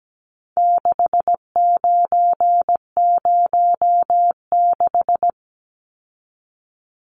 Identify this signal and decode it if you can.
Morse code, heard as a keyed tone, about 17 wpm, 700 Hz. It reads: 6906
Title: Numbers - 4-Digit 17wpm